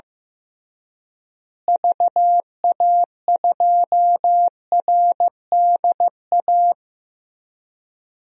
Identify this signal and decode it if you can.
Morse code, heard as a keyed tone, about 15 wpm, 700 Hz. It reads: VA2RDA
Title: DXCC-100 Call Signs - 6 Characters - Repeated 3x - 15wpm